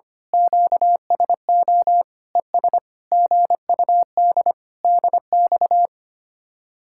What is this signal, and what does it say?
Morse code, heard as a keyed tone, about 24 wpm, 700 Hz. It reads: QSO ES GUD DX
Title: Speed-Racing Common QSO Phrases - 25wpm